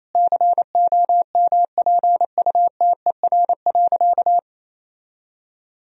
Morse code, heard as a keyed tone, about 28 wpm, 690 Hz. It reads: COMPUTER.
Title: Sentences from Top 1000 Words - Review Part 3 - 28wpm